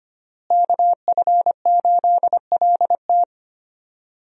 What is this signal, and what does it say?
Morse code, heard as a keyed tone, about 25 wpm, 700 Hz. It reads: KF8LT
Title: USA Call Signs - 5 Characters - Repeated 3x - 25wpm